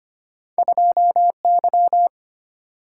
Morse code, heard as a keyed tone, about 25 wpm, 700 Hz. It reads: 2Y
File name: Two Letter-Number Combinations - 25wpm